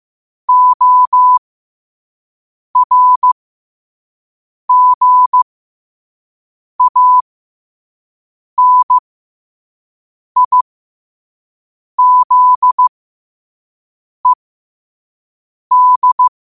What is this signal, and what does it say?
Morse code, heard as a keyed tone, about 15 wpm, spaced as for 5 wpm, 1000 Hz. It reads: ORGANIZED
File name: qotdfiles / 5wpm